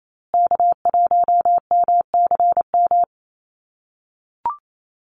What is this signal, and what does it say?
Morse code, heard as a keyed tone, about 28 wpm, 700 Hz. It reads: K1MCM
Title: USA Call Signs Repeated 3x 28wpm